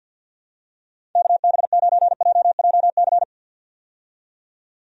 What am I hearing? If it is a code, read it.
Morse code, heard as a keyed tone, about 51 wpm, 700 Hz. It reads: KB9JJC
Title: USA Call Signs - 6 Characters - 50wpm